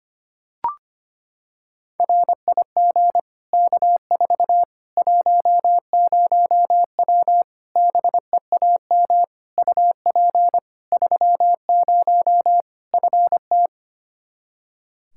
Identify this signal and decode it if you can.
Morse code, heard as a keyed tone, about 25 wpm, 700 Hz. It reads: RIG K4 10W BEAM UP 30 FT
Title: K1USN SST QSO Practice 25wpm